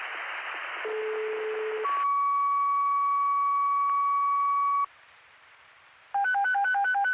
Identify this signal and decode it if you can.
Two-tone page, heard as a one-second tone, then a three-second tone, about 430 then 1130 Hz. It is heard from 0.9 to 4.9 s.